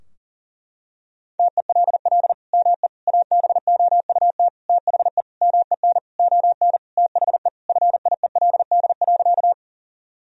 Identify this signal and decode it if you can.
Morse code, heard as a keyed tone, about 37 wpm, 700 Hz. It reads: TELL ME ABOUT THE MEN ON THE FIELD.
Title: Sentences from Top 1000 Words - Review Part 2 - 40wpm